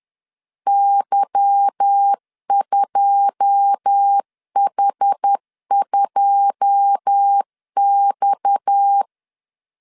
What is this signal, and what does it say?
Morse code, heard as a keyed tone, about 10 wpm, 800 Hz. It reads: Y2H2X
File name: LEZIONI / 9 2 10wpm 2736